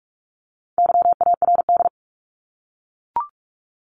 Morse code, heard as a keyed tone, about 44 wpm, 700 Hz. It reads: YARD
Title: Top 1751-2000 Words 45wpm